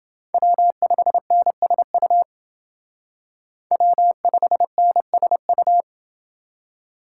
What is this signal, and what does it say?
Morse code, heard as a keyed tone, about 29 wpm, 700 Hz. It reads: W5NSU W5NSU